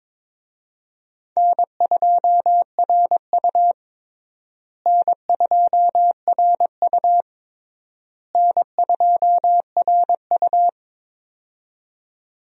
Morse code, heard as a keyed tone, about 22 wpm, 700 Hz. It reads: N2RU N2RU N2RU
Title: USA Call Signs - 4 Characters - Repeated 3x - 22wpm